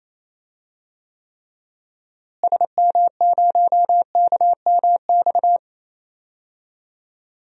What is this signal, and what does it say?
Morse code, heard as a keyed tone, about 28 wpm, 700 Hz. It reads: SM0KMX
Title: DXCC-100 Call Signs - 6 Characters - Repeated 2x - 28wpm